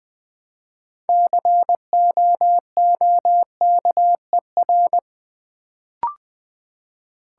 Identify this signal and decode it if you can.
Morse code, heard as a keyed tone, about 20 wpm, 700 Hz. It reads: COOKER
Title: Build A Word - 3 Letters At A Time - 6-Letter Words - 20wpm